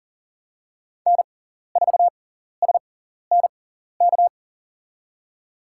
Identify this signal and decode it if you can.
Morse code, heard as a keyed tone, about 40 wpm, spaced as for 14 wpm, 700 Hz. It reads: N4SNK
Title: USA Call Signs - 6x Character Spacing - 40wpm